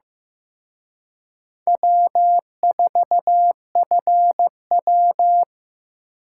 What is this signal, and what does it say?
Morse code, heard as a keyed tone, about 18 wpm, 700 Hz. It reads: W4FW
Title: Speed Racing - USA 1x2 Call Signs 15wpm